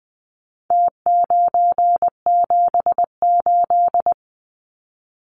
Morse code, heard as a keyed tone, about 20 wpm, 700 Hz. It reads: T978
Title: Numbers - Contest Style - 4-Digit 20wpm